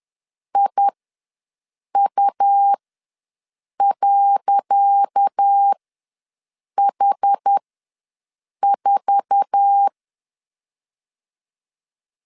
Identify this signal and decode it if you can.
Morse code, heard as a keyed tone, about 10 wpm, spaced as for 6 wpm, 800 Hz. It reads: IU.H4
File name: LEZIONI / 10 SIGNS / 10C DOT ws 10wpm 4213